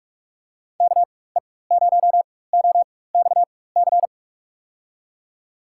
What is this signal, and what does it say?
Morse code, heard as a keyed tone, about 44 wpm, spaced as for 21 wpm, 700 Hz. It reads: KE0OXC
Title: USA Call Signs - 6 Characters - 4x Character Spacing - 45wpm